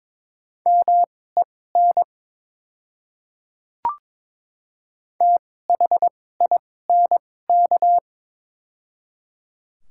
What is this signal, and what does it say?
Morse code, heard as a keyed tone, about 22 wpm, spaced as for 16 wpm, 700 Hz. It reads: MEN THINK
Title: 2x Character Spacing - Top 200 Words - 22wpm